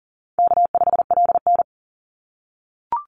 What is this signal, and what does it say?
Morse code, heard as a keyed tone, about 40 wpm, 700 Hz. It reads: K5LN
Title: Call Signs - 40wpm